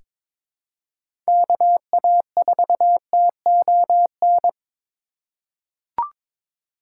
Morse code, heard as a keyed tone, about 22 wpm, 700 Hz. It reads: KA4TON
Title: USA Call Signs - 3x Character Spacing - 22wpm